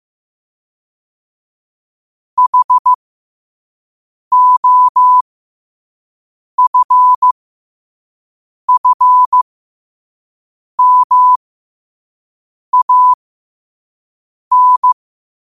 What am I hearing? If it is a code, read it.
Morse code, heard as a keyed tone, about 15 wpm, spaced as for 5 wpm, 1000 Hz. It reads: HOFFMAN
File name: qotdfiles / 5wpm